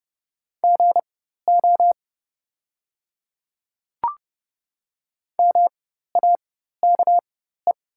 Morse code, heard as a keyed tone, about 30 wpm, spaced as for 14 wpm, 700 Hz. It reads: GO MAKE
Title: 4x Character Spacing - Top 100 Words - 30wpm